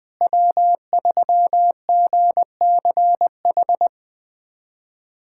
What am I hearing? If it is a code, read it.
Morse code, heard as a keyed tone, about 20 wpm, 700 Hz. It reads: W3GCH
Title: USA Callsign DE Callsign - 5 Characters - 20wpm